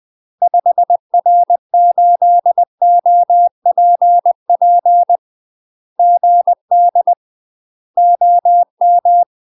Morse code, heard as a keyed tone, about 20 wpm, 700 Hz. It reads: 5R8OPP GD OM
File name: UK MASTER QSOs TRAINING / 20WPM / UK MASTER QSOs 0012